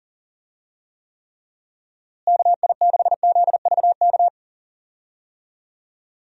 Morse code, heard as a keyed tone, about 37 wpm, 700 Hz. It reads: KI6ZVK
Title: USA Call Signs - 6 Characters - Repeated 3x - 40wpm